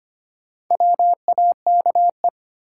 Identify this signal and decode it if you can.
Morse code, heard as a keyed tone, about 25 wpm, 700 Hz. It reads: WAKE
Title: Sets of 4 Words - New Word Drill - 25wpm